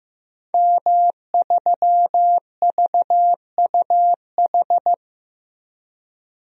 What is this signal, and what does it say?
Morse code, heard as a keyed tone, about 15 wpm, 700 Hz. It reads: M3VUH
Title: DXCC-100 Call Signs - 5 Characters - 15wpm